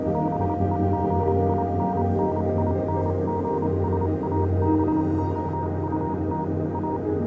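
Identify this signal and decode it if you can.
Morse code, heard as a keyed tone, about 28 wpm, spaced as for 25 wpm, 940 Hz, with noise. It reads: BEGINNINGS ABOUT A
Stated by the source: Morse code version of a fan-made pony music track.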